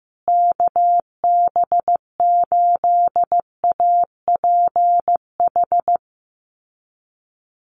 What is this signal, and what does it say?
Morse code, heard as a keyed tone, about 14 wpm, 700 Hz. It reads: KB8APH
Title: USA Call Signs Repeated 3x 15wpm